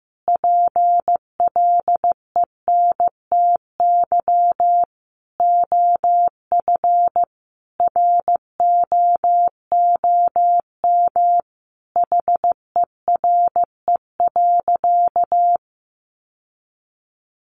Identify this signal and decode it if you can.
Morse code, heard as a keyed tone, about 15 wpm, 700 Hz. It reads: PLENTY OF ROOM HERE.
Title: Sentences from Top 1300 Words 15wpm